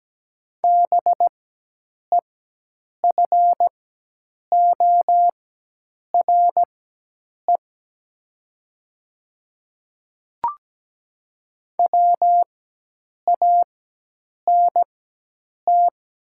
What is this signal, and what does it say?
Morse code, heard as a keyed tone, about 17 wpm, spaced as for 8 wpm, 700 Hz. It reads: BEFORE WANT
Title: 4x Character Spacing - Top 300 Words - 17wpm